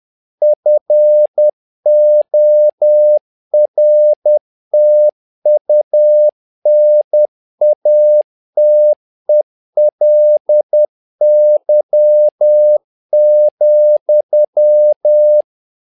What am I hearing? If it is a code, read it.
Morse code, heard as a keyed tone, about 10 wpm, 600 Hz. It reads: FORTUNATELY,
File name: learning center / tower trouble / tower trouble 10wpm